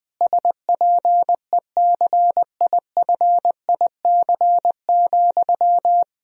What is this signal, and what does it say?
Morse code, heard as a keyed tone, about 21 wpm, 700 Hz. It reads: SPECIFIC,
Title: Sentences from Top 1000 Words - Review Part 3 - 20wpm